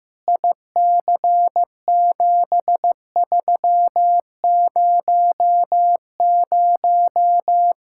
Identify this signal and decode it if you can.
Morse code, heard as a keyed tone, about 15 wpm, 700 Hz. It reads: IC7300
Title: Practice QSO - Exchange 2 - Info - 15wpm